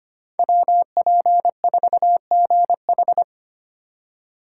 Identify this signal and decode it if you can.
Morse code, heard as a keyed tone, about 25 wpm, 700 Hz. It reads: WP4GH
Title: USA Call Signs - 5 Characters - 3x Character Spacing - 25wpm